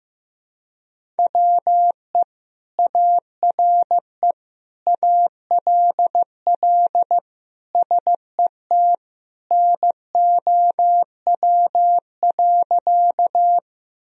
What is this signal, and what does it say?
Morse code, heard as a keyed tone, about 15 wpm, 700 Hz. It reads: WE ARE ALL SET NOW.
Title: Sentences from Top 1000 Words - Review Part 1 - 15wpm